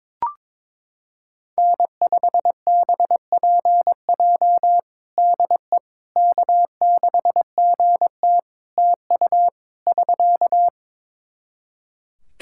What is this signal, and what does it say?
Morse code, heard as a keyed tone, about 22 wpm, 700 Hz. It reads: N5BPJ DE K6GT TU <SK>
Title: K1USN SST QSO Practice 22wpm